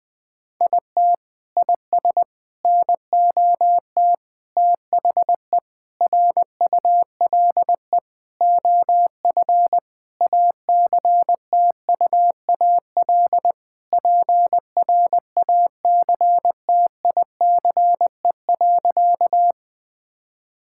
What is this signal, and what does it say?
Morse code, heard as a keyed tone, about 21 wpm, 700 Hz. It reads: IT IS NOT THE RULE OF ACTUAL PRACTICE.
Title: Sentences from Top 1000 Words - Review Part 5 - 20wpm